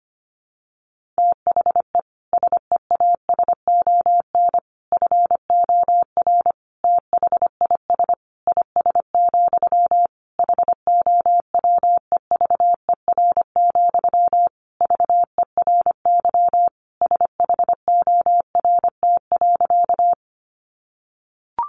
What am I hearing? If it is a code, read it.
Morse code, heard as a keyed tone, about 25 wpm, 700 Hz. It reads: THE SEASON FOR THIS IS, HOWEVER, VERY SHORT.
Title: Sentences from Top 1400 Words 25wpm